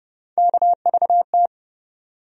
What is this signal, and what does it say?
Morse code, heard as a keyed tone, about 30 wpm, 700 Hz. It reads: KVT
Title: USA Call Sign Postfixes - 3 Letters 30wpm